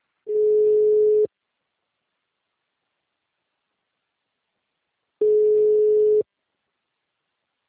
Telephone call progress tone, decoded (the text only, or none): ringback tone